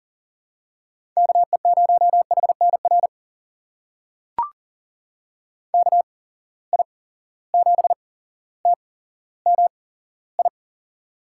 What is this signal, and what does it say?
Morse code, heard as a keyed tone, about 40 wpm, spaced as for 17 wpm, 700 Hz. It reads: KE0HNR KI7TMI
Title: USA Call Signs - 6 Characters - 8x Character Spacing - 40wpm